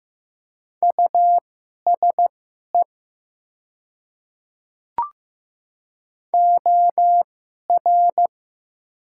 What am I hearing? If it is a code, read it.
Morse code, heard as a keyed tone, about 15 wpm, spaced as for 11 wpm, 700 Hz. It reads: USE OR
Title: 2x Character Spacing - Top 100 Words - 15wpm